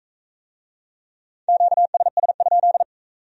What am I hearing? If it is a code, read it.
Morse code, heard as a keyed tone, about 44 wpm, 700 Hz. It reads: QSS?